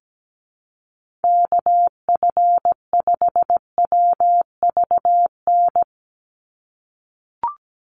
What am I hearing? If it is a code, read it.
Morse code, heard as a keyed tone, about 17 wpm, 700 Hz. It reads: KF5WVN
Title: USA Call Signs 17wpm